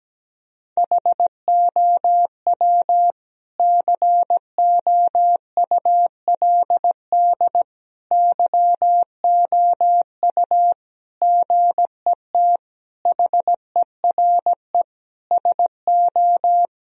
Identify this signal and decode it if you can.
Morse code, heard as a keyed tone, about 17 wpm, 700 Hz. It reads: HOW COULD YOU GET HERE SO
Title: Sentences from Top 1000 Words - Review Part 2 - 17wpm